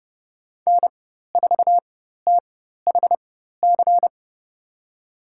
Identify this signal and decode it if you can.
Morse code, heard as a keyed tone, about 30 wpm, spaced as for 14 wpm, 700 Hz. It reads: N4THC
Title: USA Call Signs - 5 Characters - 4x Character Spacing - 30wpm